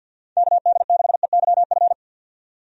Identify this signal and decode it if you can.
Morse code, heard as a keyed tone, about 47 wpm, 700 Hz. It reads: KD6EYF
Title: USA Call Signs - 6 Characters - 50wpm